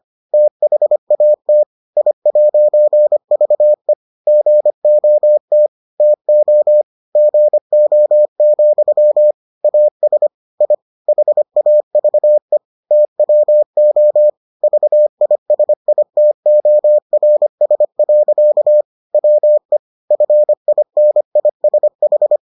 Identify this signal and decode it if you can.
Morse code, heard as a keyed tone, about 25 wpm, 600 Hz. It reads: THAT I'VE GOT TO GO, AS I HAVE TWO VISITORS. WE FINISH